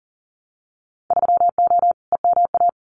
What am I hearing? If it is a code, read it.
Morse code, heard as a keyed tone, about 40 wpm, 700 Hz. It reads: 3O EMA
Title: Winter Field Day Exchanges 40wpm